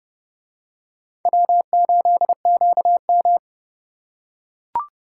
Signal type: Morse code

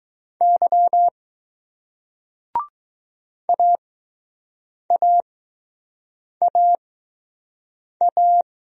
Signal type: Morse code

